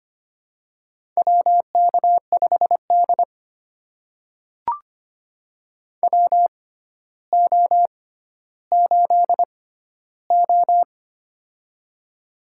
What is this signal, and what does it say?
Morse code, heard as a keyed tone, about 25 wpm, spaced as for 13 wpm, 700 Hz. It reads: WK5D WO8O
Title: USA Call Signs - 4 Characters - 6x Character Spacing - 25wpm